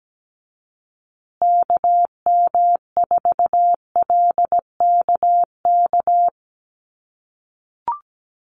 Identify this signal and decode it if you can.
Morse code, heard as a keyed tone, about 17 wpm, 700 Hz. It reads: KM4LKK
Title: USA Call Signs Repeated 2x 17wpm